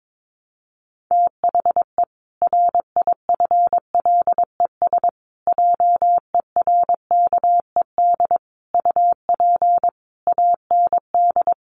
Morse code, heard as a keyed tone, about 22 wpm, 700 Hz. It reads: THE RIFLES JERKED UP AND
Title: A Sound Of Thunder 22wpm